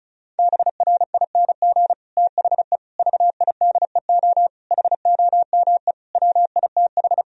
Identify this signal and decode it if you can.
Morse code, heard as a keyed tone, about 34 wpm, 700 Hz. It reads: BRING THE VIDEO HOME WITH
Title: Sentences from Top 1000 Words - Review Part 4 - 35wpm